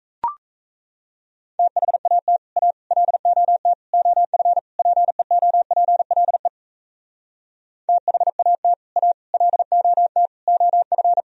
Morse code, heard as a keyed tone, about 38 wpm, 700 Hz. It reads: THAT A LOT OF PEOPLE THAT A LOT OF